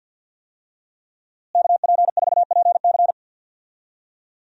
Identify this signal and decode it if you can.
Morse code, heard as a keyed tone, about 52 wpm, 700 Hz. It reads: KP4PC
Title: DXCC-100 Call Signs - 5 Characters - Repeated 3x - 50wpm